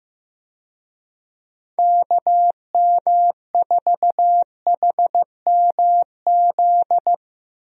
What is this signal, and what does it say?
Morse code, heard as a keyed tone, about 15 wpm, 700 Hz. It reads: KM4HMZ
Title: USA Call Signs - 6 Characters - Repeated 3x - 15wpm